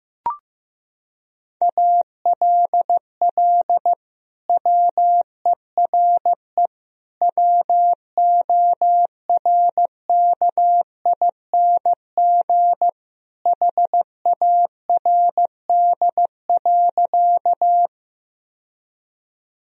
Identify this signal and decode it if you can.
Morse code, heard as a keyed tone, about 16 wpm, 700 Hz. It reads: ALL WERE WORKING HARD.
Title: Sentences from Top 1000 Words - Review Part 4 - 15wpm